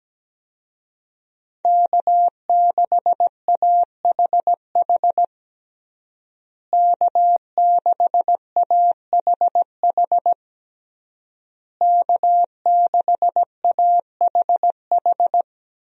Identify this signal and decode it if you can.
Morse code, heard as a keyed tone, about 17 wpm, 700 Hz. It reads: K6AHH K6AHH K6AHH
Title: USA Call Signs - 5 Characters - Repeated 3x - 17wpm